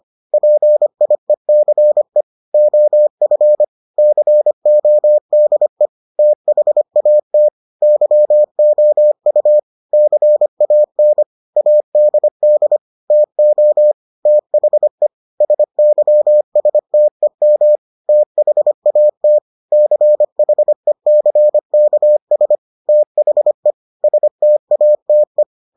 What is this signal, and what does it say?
Morse code, heard as a keyed tone, about 25 wpm, 600 Hz. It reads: PIECE OF CODE THAT YOU CAN ADD TO THE SYSTEM THAT CHECKS THE STATE